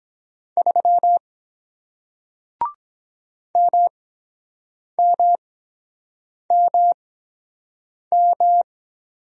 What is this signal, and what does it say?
Morse code, heard as a keyed tone, about 23 wpm, spaced as for 7 wpm, 700 Hz. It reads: 3 MMMM